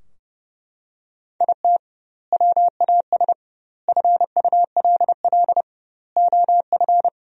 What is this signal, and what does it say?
Morse code, heard as a keyed tone, about 30 wpm, 700 Hz. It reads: IT WAS FULL OF
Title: 2x Word-Spacing - Sentences from Top 500 Words - 30wpm